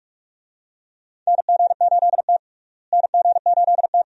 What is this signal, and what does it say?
Morse code, heard as a keyed tone, about 42 wpm, 700 Hz. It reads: NG8T NG8T
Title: USA Call Signs - 4 Characters - Repeated 2x - 45wpm